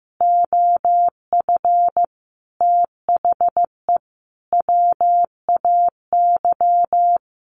Morse code, heard as a keyed tone, about 15 wpm, 700 Hz. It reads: OF THE WAY
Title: Sets of 5 Words Encore2 15wpm